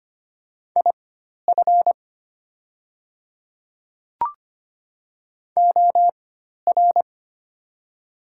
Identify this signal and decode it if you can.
Morse code, heard as a keyed tone, about 25 wpm, spaced as for 12 wpm, 700 Hz. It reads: IF OR
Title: 4x Character Spacing - Top 100 Words - 25wpm